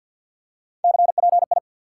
Morse code, heard as a keyed tone, about 54 wpm, 700 Hz. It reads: CPI
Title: Top 90 QSO Elements 50wpm